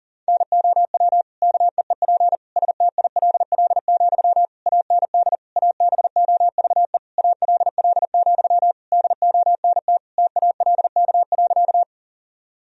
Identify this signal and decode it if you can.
Morse code, heard as a keyed tone, about 37 wpm, 700 Hz. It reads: NOW KEEP STILL, AND ABOVE ALL, DONT TALK.
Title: Sentences from Top 1000 Words - Review Part 2 - 40wpm